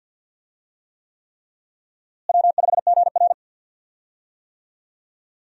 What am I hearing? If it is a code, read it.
Morse code, heard as a keyed tone, about 52 wpm, 700 Hz. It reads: W5GR
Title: USA Call Signs - 4 Characters - Repeated 3x - 50wpm